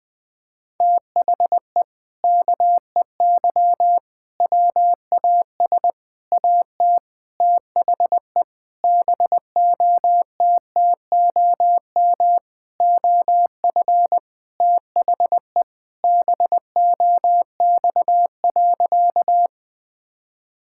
Morse code, heard as a keyed tone, about 20 wpm, 700 Hz. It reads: THE KEY WAS AT THE BOTTOM OF THE BOX.
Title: Sentences with 4-Word Phrases - Spoken Beginning - 20wpm